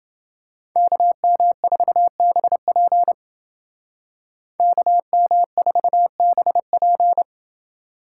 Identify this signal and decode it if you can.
Morse code, heard as a keyed tone, about 29 wpm, 700 Hz. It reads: KM4BP KM4BP